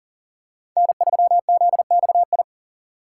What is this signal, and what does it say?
Morse code, heard as a keyed tone, about 40 wpm, 700 Hz. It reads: N3ZXI
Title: USA Call Signs - 5 Characters - 3x Character Spacing - 40wpm